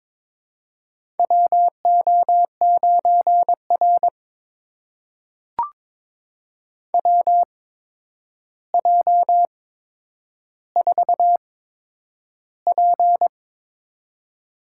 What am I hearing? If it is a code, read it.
Morse code, heard as a keyed tone, about 22 wpm, spaced as for 9 wpm, 700 Hz. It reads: WO9R WJ4P
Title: USA Call Signs - 4 Characters - 8x Character Spacing - 22wpm